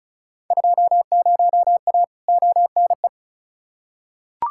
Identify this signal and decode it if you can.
Morse code, heard as a keyed tone, about 36 wpm, 700 Hz. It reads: 20A ONE
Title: Field Day Exchanges Encore - 35wpm